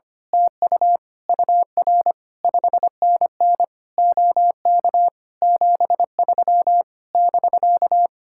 Morse code, heard as a keyed tone, about 24 wpm, 700 Hz. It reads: TU UR 5NN OK 73 <BK>
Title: Speed-Racing Common QSO Phrases - 25wpm